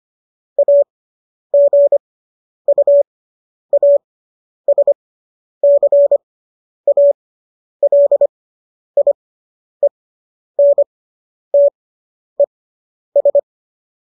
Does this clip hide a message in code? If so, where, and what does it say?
Yes: Morse code from 0.6 to 13.4 s, heard as a keyed tone, about 25 wpm, spaced as for 10 wpm, 570 Hz. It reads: AGUASCALIENTES